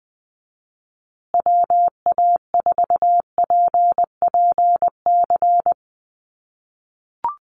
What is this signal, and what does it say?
Morse code, heard as a keyed tone, about 20 wpm, 700 Hz. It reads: WA4PPC
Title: USA Call Signs Repeated 2x 20wpm